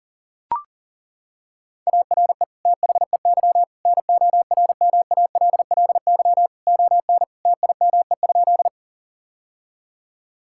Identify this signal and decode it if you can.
Morse code, heard as a keyed tone, about 37 wpm, 700 Hz. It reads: ARE THEY NORMALLY ON TIME?
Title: Sentences from Top 1000 Words - Review Part 6 - 40wpm